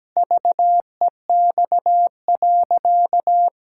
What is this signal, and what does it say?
Morse code, heard as a keyed tone, about 17 wpm, 700 Hz. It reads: VEX.
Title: Pangram Challenge v2 - 17wpm